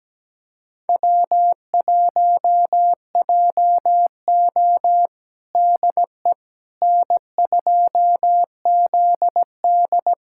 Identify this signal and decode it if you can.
Morse code, heard as a keyed tone, about 17 wpm, 700 Hz. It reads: W1JO DE N2ZD
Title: USA Callsign DE Callsign - 4 Characters - 17wpm